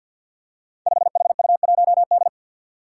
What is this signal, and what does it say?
Morse code, heard as a keyed tone, about 47 wpm, 700 Hz. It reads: 5HU1D
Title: Five Challenging Letter-Number Combinations 50wpm